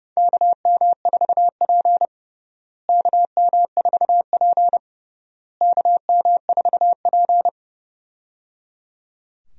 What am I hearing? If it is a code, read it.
Morse code, heard as a keyed tone, about 29 wpm, 700 Hz. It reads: KM4P KM4P KM4P
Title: USA Call Signs - 4 Characters - Repeated 3x - 30wpm